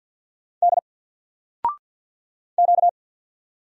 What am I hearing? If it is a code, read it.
Morse code, heard as a keyed tone, about 50 wpm, 700 Hz. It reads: D Q
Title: Single Letter-Number 50wpm